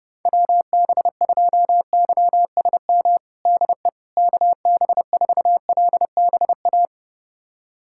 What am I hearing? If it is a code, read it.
Morse code, heard as a keyed tone, about 30 wpm, 700 Hz. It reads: WB2YSM DE KB4LBA
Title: USA Callsign DE Callsign 30wpm